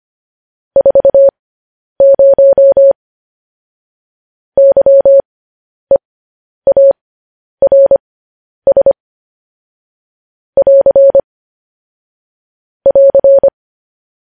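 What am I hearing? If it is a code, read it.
Morse code, heard as a keyed tone, about 25 wpm, spaced as for 10 wpm, 550 Hz. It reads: 40 YEARS <AR> <AR>
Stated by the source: Morse news headlines at 10.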